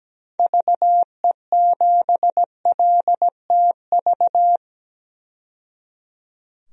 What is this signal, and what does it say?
Morse code, heard as a keyed tone, about 17 wpm, 700 Hz. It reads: VE7LTV
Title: DXCC-100 Call Signs - 6 Characters - 17wpm